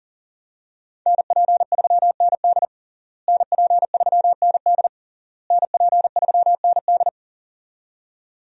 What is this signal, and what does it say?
Morse code, heard as a keyed tone, about 40 wpm, 700 Hz. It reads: NP3ND NP3ND NP3ND
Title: USA Call Signs - 5 Characters - Repeated 3x - 40wpm